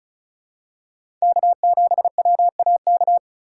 Morse code, heard as a keyed tone, about 34 wpm, 700 Hz. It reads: K7WAK
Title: USA Call Signs - 2x Character Spacing - 35wpm